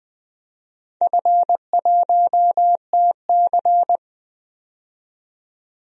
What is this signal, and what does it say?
Morse code, heard as a keyed tone, about 20 wpm, 700 Hz. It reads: F1TC
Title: DXCC-100 Call Signs - 4 Characters - 20wpm